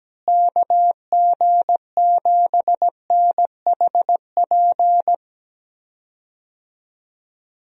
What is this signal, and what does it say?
Morse code, heard as a keyed tone, about 17 wpm, 700 Hz. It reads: KG7NHP
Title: USA Call Signs - 6 Characters - Repeated 3x - 17wpm